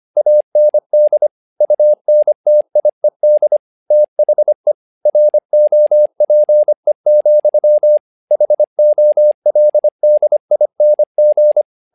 Morse code, heard as a keyed tone, about 25 wpm, 600 Hz. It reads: AND UNTIED THE ROPE, HOLDING
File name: learning center / tower trouble / tower trouble 25wpm